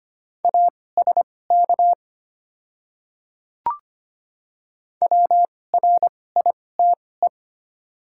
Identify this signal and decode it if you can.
Morse code, heard as a keyed tone, about 25 wpm, spaced as for 18 wpm, 700 Hz. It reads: ASK WRITE
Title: 2x Character Spacing - Top 300 Words - 25wpm